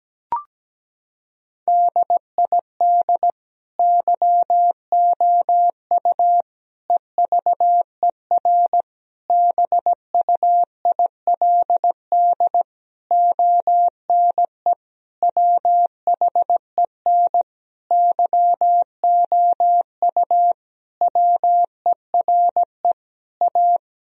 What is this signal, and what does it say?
Morse code, heard as a keyed tone, about 17 wpm, 700 Hz. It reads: DID YOU EVER BUILD ONE WHEN YOU WERE A
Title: Sentences from Top 1000 Words - Review Part 1 - 17wpm